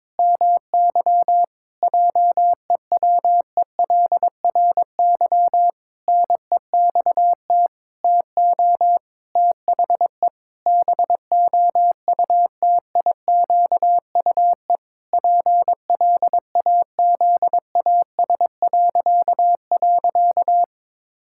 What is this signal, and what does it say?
Morse code, heard as a keyed tone, about 22 wpm, 700 Hz. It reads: MY JEWELRY NEXT TO THE BOUTIQUE PLAZAS..
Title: Pangram Challenge v2 - 22wpm